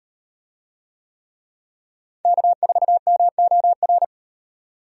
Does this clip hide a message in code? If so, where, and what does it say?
Yes: Morse code from 2.3 to 4.0 s, heard as a keyed tone, about 38 wpm, 700 Hz. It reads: K4MOR